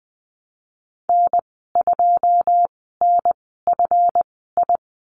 Morse code, heard as a keyed tone, about 20 wpm, spaced as for 15 wpm, 700 Hz. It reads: N2NFI
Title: 245 Lesson 39g - Letter z - Call Signs 20wpm